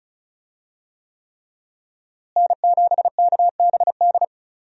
Morse code, heard as a keyed tone, about 36 wpm, 700 Hz. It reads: N7KBD
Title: USA Call Signs - 5 Characters - Repeated 3x - 35wpm